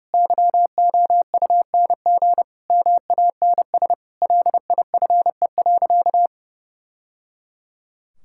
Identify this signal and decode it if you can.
Morse code, heard as a keyed tone, about 30 wpm, 700 Hz. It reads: YOUNG MANS LIFE.
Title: Sentences from Top 1000 Words - Review Part 6 - 30wpm